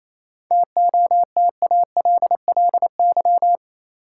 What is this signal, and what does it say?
Morse code, heard as a keyed tone, about 28 wpm, 700 Hz. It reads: TOTALLY
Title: Sentences from Top 1600 Words - New Word Drill Alt - 28wpm